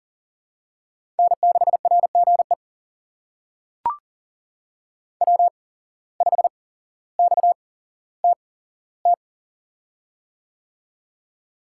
Morse code, heard as a keyed tone, about 38 wpm, spaced as for 17 wpm, 700 Hz. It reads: N6RGE W5XTT
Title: USA Call Signs - 5 Characters - 8x Character Spacing - 40wpm